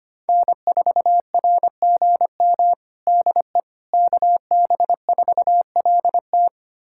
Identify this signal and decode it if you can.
Morse code, heard as a keyed tone, about 25 wpm, 700 Hz. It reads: N4RGM DE KB4LT
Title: USA Callsign DE Callsign - 5 Characters - 25wpm